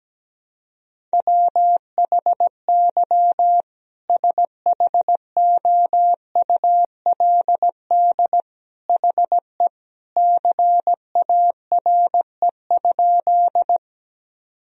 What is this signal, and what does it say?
Morse code, heard as a keyed tone, about 17 wpm, 700 Hz. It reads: WHY SHOULD HE CARE?
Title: Sentences from Top 1000 Words - Review Part 1 - 17wpm